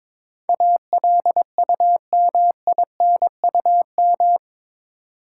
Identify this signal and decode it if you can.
Morse code, heard as a keyed tone, about 22 wpm, 700 Hz. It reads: ALUMINUM
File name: practice / 22wpm / Eight Letter Words - 22wpm